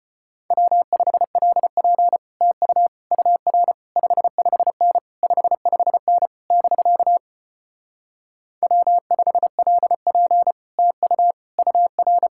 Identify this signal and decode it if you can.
Morse code, heard as a keyed tone, about 33 wpm, 700 Hz. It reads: W5LP TU UR 55N 55N <BK> W5LP TU UR